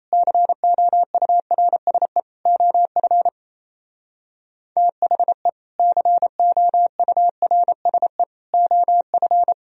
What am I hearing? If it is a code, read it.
Morse code, heard as a keyed tone, about 34 wpm, 700 Hz. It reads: COURSE OF THE COURSE OF
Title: Speed-Racing - Sets of 3 Words 28wpm